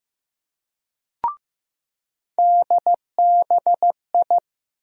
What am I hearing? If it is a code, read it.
Morse code, heard as a keyed tone, about 15 wpm, 700 Hz. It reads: DBI
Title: USA Call Sign Postfixes - 3 Letters 15wpm